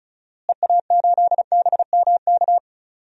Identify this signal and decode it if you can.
Morse code, heard as a keyed tone, about 36 wpm, 700 Hz. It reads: EA8BMK
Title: DXCC-100 Call Signs - 6 Characters - Repeated 2x - 35wpm